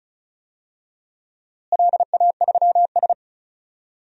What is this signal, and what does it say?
Morse code, heard as a keyed tone, about 36 wpm, 700 Hz. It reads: LA3S
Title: DXCC-100 Call Signs - 4 Characters - Repeated 2x - 35wpm